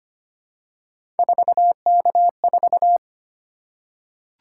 Morse code, heard as a keyed tone, about 25 wpm, 700 Hz. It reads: 4K4
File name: Three Letter-Number Combinations - 25wpm